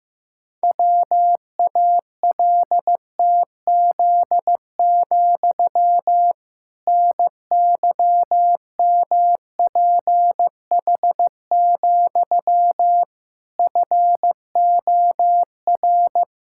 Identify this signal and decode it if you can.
Morse code, heard as a keyed tone, about 15 wpm, 700 Hz. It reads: WALTZ, NYMPH, FOR
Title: Pangram Challenge v2 - 15wpm